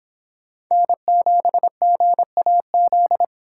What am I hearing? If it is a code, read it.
Morse code, heard as a keyed tone, about 26 wpm, 700 Hz. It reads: N7GAZ